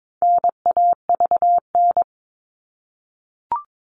Morse code, heard as a keyed tone, about 22 wpm, 700 Hz. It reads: NAVN
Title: All Cut Numbers - 4-Digit 22wpm